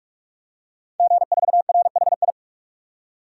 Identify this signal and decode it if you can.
Morse code, heard as a keyed tone, about 44 wpm, 700 Hz. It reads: G4RHI
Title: DXCC-100 Call Signs - 5 Characters - 45wpm